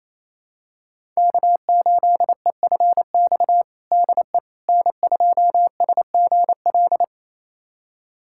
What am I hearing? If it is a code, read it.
Morse code, heard as a keyed tone, about 28 wpm, 700 Hz. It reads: K8EFX DE N2SGL
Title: USA Callsign DE Callsign - 5 Characters - 28wpm